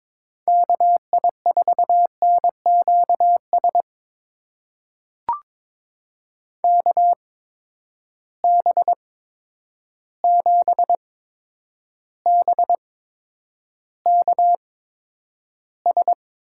Morse code, heard as a keyed tone, about 22 wpm, spaced as for 9 wpm, 700 Hz. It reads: KI4NQS KB7BKS
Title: USA Call Signs - 6 Characters - 8x Character Spacing - 22wpm